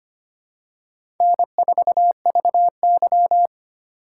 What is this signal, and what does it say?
Morse code, heard as a keyed tone, about 25 wpm, 700 Hz. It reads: N4VY
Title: USA Call Signs - 4 Characters - 4x Character Spacing - 25wpm